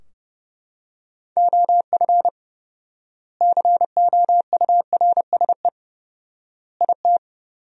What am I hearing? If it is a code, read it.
Morse code, heard as a keyed tone, about 30 wpm, 700 Hz. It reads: OF COURSE IT
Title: 4x Word-Spacing - Sentences from Top 500 Words - 30wpm